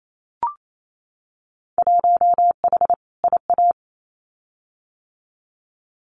Morse code, heard as a keyed tone, about 28 wpm, 700 Hz. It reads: 1H IA
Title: Winter Field Day Exchanges 28wpm